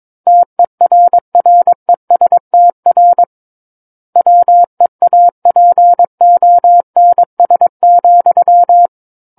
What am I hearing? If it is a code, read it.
Morse code, heard as a keyed tone, about 22 wpm, 700 Hz. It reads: TERRESTR WEAPONS,
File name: Book One / 13-Chapter 13 - 22 WPM